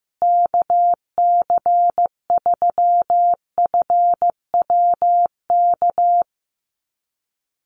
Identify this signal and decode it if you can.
Morse code, heard as a keyed tone, about 15 wpm, 700 Hz. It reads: KC3FWK
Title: USA Call Signs Repeated 2x 15wpm